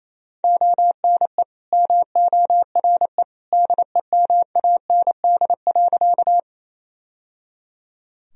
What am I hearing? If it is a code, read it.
Morse code, heard as a keyed tone, about 27 wpm, 700 Hz. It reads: ONE MORE DEMAND.
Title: Sentences from Top 1000 Words - Review Part 4 - 28wpm